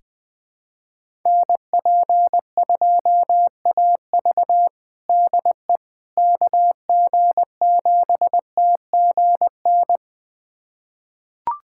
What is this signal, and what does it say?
Morse code, heard as a keyed tone, about 20 wpm, 700 Hz. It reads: NP2AV DE KG7TGN
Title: USA Callsign DE Callsign 20wpm